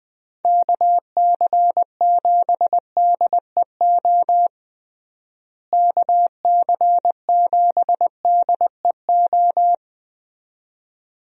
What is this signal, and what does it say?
Morse code, heard as a keyed tone, about 20 wpm, 700 Hz. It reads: KC7DEO KC7DEO
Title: USA Call Signs - 6 Characters - Repeated 2x - 20wpm